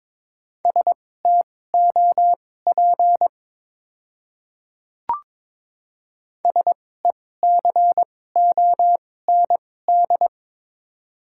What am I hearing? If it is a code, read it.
Morse code, heard as a keyed tone, about 22 wpm, spaced as for 16 wpm, 700 Hz. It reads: STOP SECOND
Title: 2x Character Spacing - Top 300 Words - 22wpm